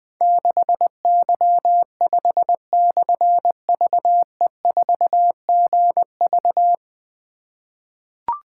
Morse code, heard as a keyed tone, about 20 wpm, 700 Hz. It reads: 6Y5/VE4GV
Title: Difficult Call Signs 20wpm